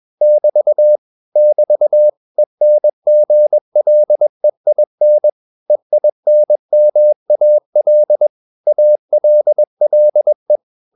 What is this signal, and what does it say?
Morse code, heard as a keyed tone, about 21 wpm, 600 Hz. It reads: <BT> <BT> ENGLEIN EINMAL ALLE